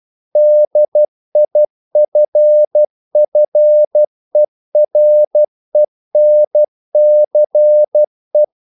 Morse code, heard as a keyed tone, about 12 wpm, 600 Hz. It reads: DIFFERENCE
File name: CW / Practice / EnglishQuotations0112